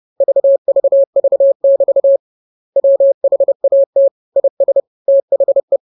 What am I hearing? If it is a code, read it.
Morse code, heard as a keyed tone, about 30 wpm, 550 Hz. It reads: VVV<BT> WHAT IS THE
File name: jokes / joke 13 30 0000